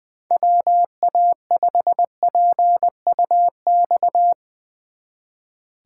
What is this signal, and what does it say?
Morse code, heard as a keyed tone, about 20 wpm, 700 Hz. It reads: WA5PUX
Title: Speed Racing - USA 2x3 Call Signs 20wpm